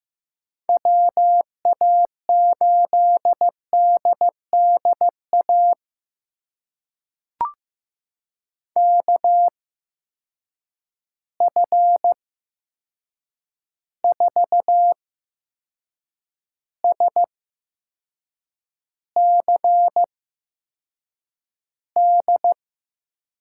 Morse code, heard as a keyed tone, about 15 wpm, spaced as for 6 wpm, 700 Hz. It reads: WA8DDA KF4SCD
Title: USA Call Signs - 8x Character Spacing - 15wpm